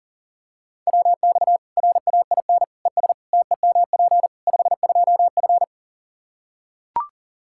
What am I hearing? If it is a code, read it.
Morse code, heard as a keyed tone, about 40 wpm, 700 Hz. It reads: WX RAIN ES TEMP 52F
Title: Practice QSO - Exchange 2 - Info - 40wpm